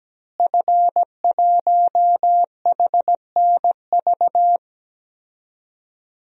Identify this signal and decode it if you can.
Morse code, heard as a keyed tone, about 17 wpm, 700 Hz. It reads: F1HNV
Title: DXCC-100 Call Signs - 5 Characters - Repeated 2x - 17wpm